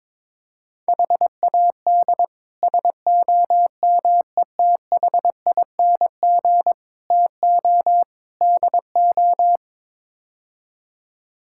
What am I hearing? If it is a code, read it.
Morse code, heard as a keyed tone, about 22 wpm, 700 Hz. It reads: HAD SOMETHING TO DO
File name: practice / 22wpm / Build A Phrase - 5-Word Phrases - 22wpm